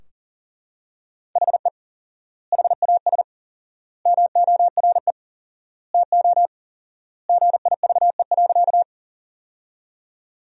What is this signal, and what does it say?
Morse code, heard as a keyed tone, about 37 wpm, 700 Hz. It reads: HE HAS MORE TO GIVE.
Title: 4x Word-Spacing - Sentences from Top 200 Words - 40wpm